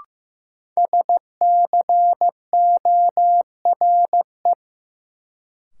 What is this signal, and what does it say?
Morse code, heard as a keyed tone, about 15 wpm, 700 Hz. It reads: SCORE
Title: Sentences from Top 1600 Words - New Word Drill Alt - 15wpm